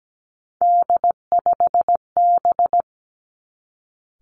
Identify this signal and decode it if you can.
Morse code, heard as a keyed tone, about 17 wpm, 700 Hz. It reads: D5B
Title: Three Challenging Letter-Number Combinations 17wpm